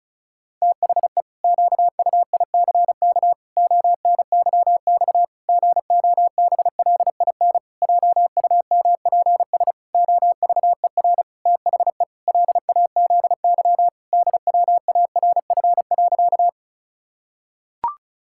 Morse code, heard as a keyed tone, about 34 wpm, 700 Hz. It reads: THE QUICK ONYX GOBLIN JUMPS OVER THE LAZY DWARF.
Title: Pangram Challenge v2 - 35wpm